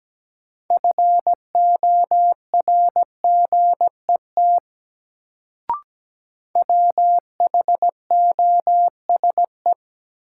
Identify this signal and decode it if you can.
Morse code, heard as a keyed tone, about 17 wpm, 700 Hz. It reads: FORGET WHOSE
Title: Sentences from Top 1200 Words - New Word Drill Alt - 17wpm